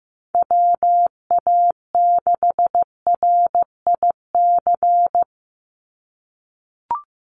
Morse code, heard as a keyed tone, about 15 wpm, 700 Hz. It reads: WA6RIC
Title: NA SOTA Chaser Callsigns 15wpm